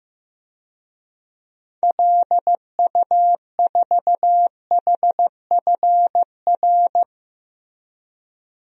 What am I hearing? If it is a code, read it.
Morse code, heard as a keyed tone, about 15 wpm, 700 Hz. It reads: LU4HFR
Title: DXCC-100 Call Signs - 6 Characters - 15wpm